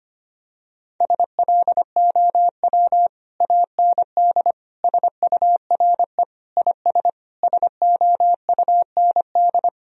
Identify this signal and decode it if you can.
Morse code, heard as a keyed tone, about 25 wpm, 700 Hz. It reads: SLOW AND SURE IS SOUND
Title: Sentences from Top 1000 Words - Review Part 2 - 25wpm